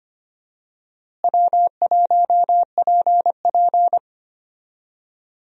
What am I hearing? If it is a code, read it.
Morse code, heard as a keyed tone, about 25 wpm, 700 Hz. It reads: W1PP
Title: USA Call Signs - 4 Characters - Repeated 2x - 25wpm